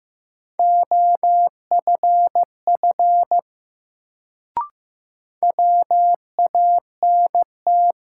Morse code, heard as a keyed tone, about 15 wpm, 700 Hz. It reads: OFF WANT
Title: Sentences from Top 200 Words - New Word Drill - 15wpm